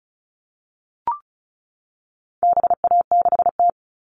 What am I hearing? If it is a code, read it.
Morse code, heard as a keyed tone, about 34 wpm, 700 Hz. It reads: BA6T
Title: All Cut Numbers - 4-Digit 35wpm